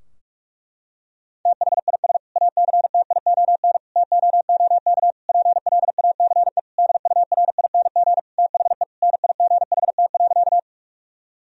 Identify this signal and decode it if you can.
Morse code, heard as a keyed tone, about 46 wpm, 700 Hz. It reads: THIS ACTION TOOK PLACE DURING THE NIGHT.
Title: Sentences from Top 1000 Words - Review Part 4 - 45wpm